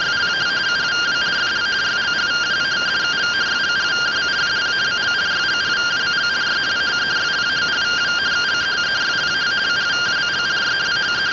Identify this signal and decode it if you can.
Two-tone teleprinter signal, about 170 Hz shift.